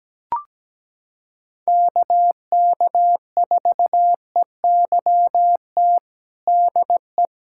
Morse code, heard as a keyed tone, about 17 wpm, 700 Hz. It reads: KK4EYT DE
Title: USA Callsign DE Callsign - 6 Characters - 17wpm